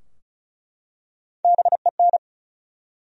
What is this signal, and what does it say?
Morse code, heard as a keyed tone, about 36 wpm, 700 Hz. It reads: BEN
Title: USA Common Names - Max 4 Letters - 35wpm